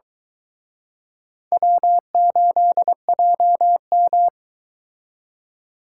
Morse code, heard as a keyed tone, about 23 wpm, 700 Hz. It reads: W8JM